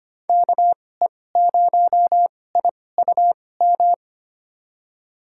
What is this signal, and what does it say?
Morse code, heard as a keyed tone, about 25 wpm, spaced as for 18 wpm, 700 Hz. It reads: KE0IUM
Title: USA Call Signs - 6 Characters - 2x Character Spacing - 25wpm